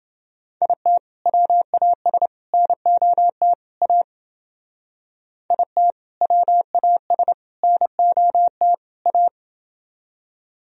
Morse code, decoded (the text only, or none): IT WAS NOT A IT WAS NOT A